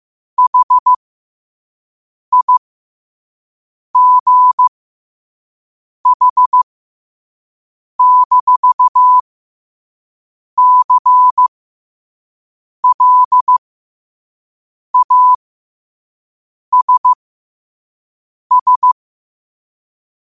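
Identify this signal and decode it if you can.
Morse code, heard as a keyed tone, about 15 wpm, spaced as for 5 wpm, 1000 Hz. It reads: HIGH-CLASS
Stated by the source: Morse code sent at 5 words per minute.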